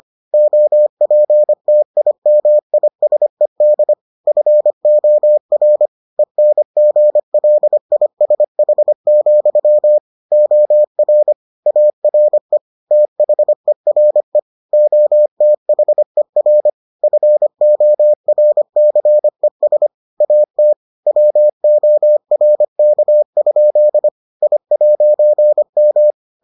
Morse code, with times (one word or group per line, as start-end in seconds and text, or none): 0.3-3.9 OPTIMISED
4.3-5.9 FOR
6.2-10.0 ENGLISH,
10.3-11.3 OR
11.7-12.6 ARE
12.9-14.4 THERE
14.7-16.7 OTHER
17.0-19.9 FORCES
20.2-20.7 AT
21.1-24.1 WORK?
24.4-26.1 I'M